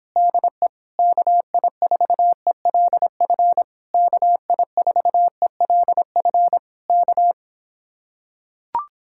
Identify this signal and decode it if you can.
Morse code, heard as a keyed tone, about 26 wpm, 700 Hz. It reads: DE KI4ELF KI4ELF K